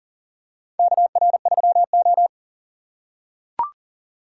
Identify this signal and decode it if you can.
Morse code, heard as a keyed tone, about 40 wpm, 700 Hz. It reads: KR3O
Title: USA Call Signs - 4 Characters - 4x Character Spacing - 40wpm